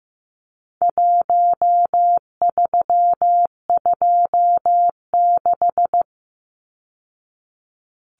Morse code, heard as a keyed tone, about 15 wpm, 700 Hz. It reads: 1326
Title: Numbers - 4-Digit 15wpm